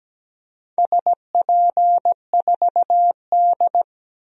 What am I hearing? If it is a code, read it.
Morse code, heard as a keyed tone, about 17 wpm, 700 Hz. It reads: SP4D
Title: DXCC-100 Call Signs - 4 Characters - 17wpm